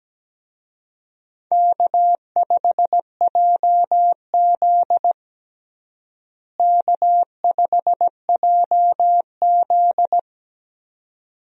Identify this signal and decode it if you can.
Morse code, heard as a keyed tone, about 17 wpm, 700 Hz. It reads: K5JZ K5JZ
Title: USA Call Signs - 4 Characters - Repeated 2x - 17wpm